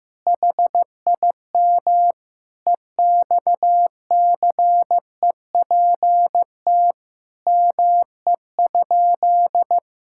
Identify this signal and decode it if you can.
Morse code, heard as a keyed tone, about 15 wpm, 700 Hz. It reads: HIM EXCEPT ME?
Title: Sentences from Top 1600 Words 15wpm